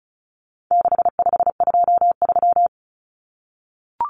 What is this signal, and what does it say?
Morse code, heard as a keyed tone, about 34 wpm, 700 Hz. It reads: 6523
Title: Numbers - 4-Digit 35wpm